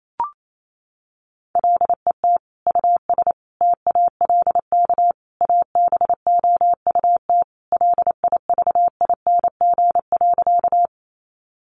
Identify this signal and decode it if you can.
Morse code, heard as a keyed tone, about 28 wpm, 700 Hz. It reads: LET US TALK ABOUT LIVING.
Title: Sentences from Top 1100 Words 28wpm